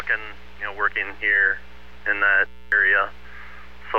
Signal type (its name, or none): Emergency Alert System (SAME) tones